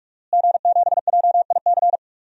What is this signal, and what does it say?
Morse code, heard as a keyed tone, about 45 wpm, 700 Hz. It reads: G7JIC
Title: DXCC-100 Call Signs - 5 Characters - Repeated 2x - 45wpm